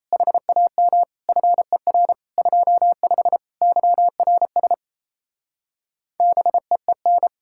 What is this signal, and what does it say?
Morse code, heard as a keyed tone, about 34 wpm, 700 Hz. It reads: HAM FER 25 YRS BEEN
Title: Speed-Racing Common QSO Phrases - 28wpm